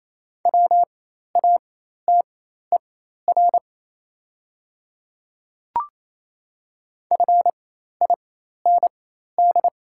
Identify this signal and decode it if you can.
Morse code, heard as a keyed tone, about 28 wpm, spaced as for 13 wpm, 690 Hz. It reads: WATER FIND
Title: 4x Character Spacing - Top 200 Words - 28wpm